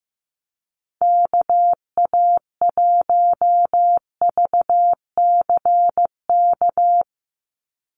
Morse code, heard as a keyed tone, about 15 wpm, 700 Hz. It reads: KA1VCK
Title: USA Call Signs 15wpm